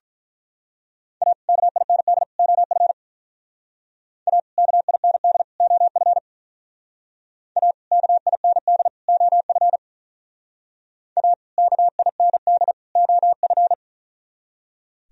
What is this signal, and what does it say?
Morse code, heard as a keyed tone, about 44 wpm, 700 Hz. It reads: A KIND OF A KIND OF A KIND OF A KIND OF